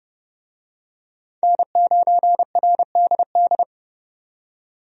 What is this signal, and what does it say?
Morse code, heard as a keyed tone, about 30 wpm, 700 Hz. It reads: N9RDD
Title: USA Call Signs - 5 Characters - 30wpm